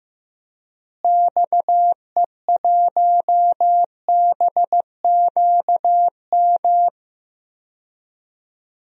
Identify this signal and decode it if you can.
Morse code, heard as a keyed tone, about 15 wpm, 700 Hz. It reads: XE1BQM
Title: DXCC-100 Call Signs - 6 Characters - Repeated 2x - 15wpm